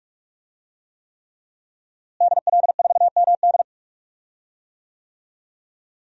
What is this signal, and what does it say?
Morse code, heard as a keyed tone, about 41 wpm, 700 Hz. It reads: DL4MD
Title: DXCC-100 Call Signs - 5 Characters - Repeated 3x - 45wpm